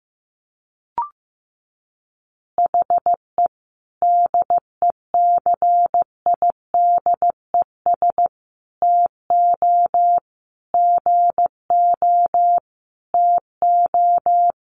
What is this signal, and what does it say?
Morse code, heard as a keyed tone, about 15 wpm, 700 Hz. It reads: HE DECIDES TO GO TO
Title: Easy Phrases 15wpm